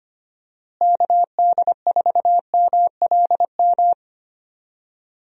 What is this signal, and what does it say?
Morse code, heard as a keyed tone, about 25 wpm, 700 Hz. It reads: KD4MLM
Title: USA Call Signs - 6 Characters - Repeated 2x - 25wpm